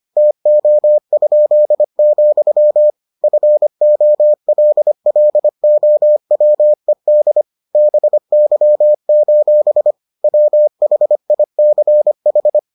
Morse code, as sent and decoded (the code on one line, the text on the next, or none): - --- ..--.. --..-- / ..-. --- .-.. .-.. --- .-- . -.. / -... -.-- ---... / .-- .... .. -.-. ....
TO?, FOLLOWED BY: WHICH